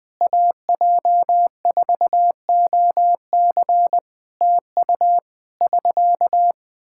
Morse code, heard as a keyed tone, about 20 wpm, 700 Hz. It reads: AJ4OC TU <SK>
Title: Practice QSO - Exchange 4 - Ending - 20wpm